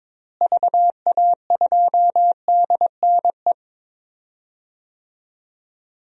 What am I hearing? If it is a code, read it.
Morse code, heard as a keyed tone, about 23 wpm, 700 Hz. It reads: VA2DNE
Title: DXCC-100 Call Signs - 6 Characters - Repeated 2x - 22wpm